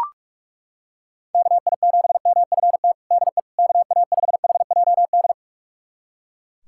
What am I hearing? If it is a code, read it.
Morse code, heard as a keyed tone, about 41 wpm, 700 Hz. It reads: KI7MFT DE KA5HJD
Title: USA Callsign DE Callsign - 6 Characters - 45wpm